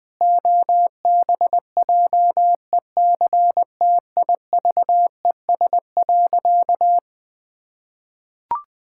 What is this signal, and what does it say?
Morse code, heard as a keyed tone, about 20 wpm, 700 Hz. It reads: OBJECTIVES.
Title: Pangram Challenge v2 - 20wpm